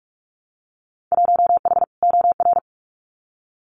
Morse code, heard as a keyed tone, about 44 wpm, 700 Hz. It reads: 1H OR
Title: Winter Field Day Exchanges 45wpm